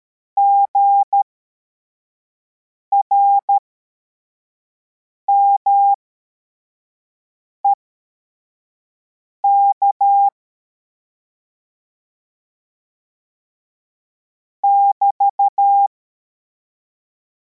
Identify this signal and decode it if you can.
Morse code, heard as a keyed tone, about 13 wpm, spaced as for 4 wpm, 800 Hz. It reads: GRMEK <BT>
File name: formation / cw / Text41